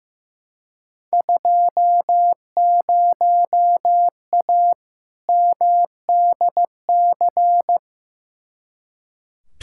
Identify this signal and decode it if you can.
Morse code, heard as a keyed tone, about 15 wpm, 700 Hz. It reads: 20A MDC
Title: Field Day Exchanges Encore - 15wpm